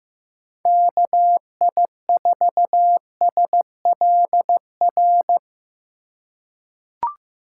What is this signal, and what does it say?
Morse code, heard as a keyed tone, about 15 wpm, 700 Hz. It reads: KI4SLR
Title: USA Call Signs - 6 Characters - 2x Character Spacing - 15wpm